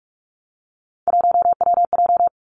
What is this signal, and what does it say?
Morse code, heard as a keyed tone, about 46 wpm, 700 Hz. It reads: 1WJ
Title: Three Challenging Letter-Number Combinations 45wpm